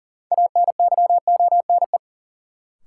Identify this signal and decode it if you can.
Morse code, heard as a keyed tone, about 40 wpm, 700 Hz. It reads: ANYONE
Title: Sentences from Top 900 Words - New Word Drill - 40wpm